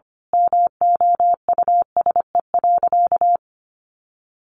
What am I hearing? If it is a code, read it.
Morse code, heard as a keyed tone, about 25 wpm, 700 Hz. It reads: MOUSE.
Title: A Sound Of Thunder 25wpm